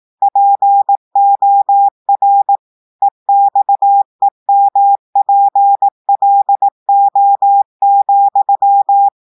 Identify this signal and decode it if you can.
Morse code, heard as a keyed tone, about 18 wpm, 800 Hz. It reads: POR EXEMPLO,
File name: audios / portugais / portugais texte en clair 2 vitesse 18 mots minutes